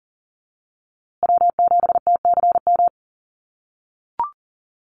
Morse code, heard as a keyed tone, about 40 wpm, 700 Hz. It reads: W7TCM
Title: USA Call Signs Repeated 2x 40wpm